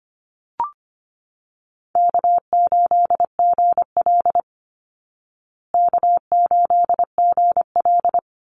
Morse code, heard as a keyed tone, about 25 wpm, 700 Hz. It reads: K8GL K8GL
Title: Easy Call Signs Repeated 3x - 25wpm